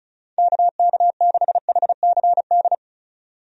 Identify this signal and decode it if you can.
Morse code, heard as a keyed tone, about 36 wpm, 700 Hz. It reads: KK6HCD
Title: USA Call Signs - 6 Characters - Repeated 3x - 35wpm